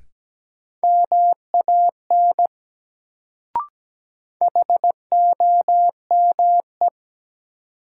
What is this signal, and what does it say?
Morse code, heard as a keyed tone, about 17 wpm, 700 Hz. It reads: MAN HOME
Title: Sentences from Top 200 Words - New Word Drill - 17wpm